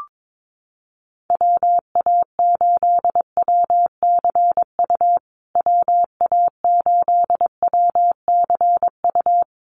Morse code, heard as a keyed tone, about 22 wpm, 700 Hz. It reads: WA8WCU WA8WCU
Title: USA Call Signs Repeated 3x 22wpm